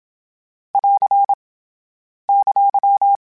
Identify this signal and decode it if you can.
Morse code, heard as a keyed tone, about 27 wpm, 800 Hz. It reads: <AR> !
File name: Q Calls / Q Calls-mp3s / Q Calls SPSL2 @25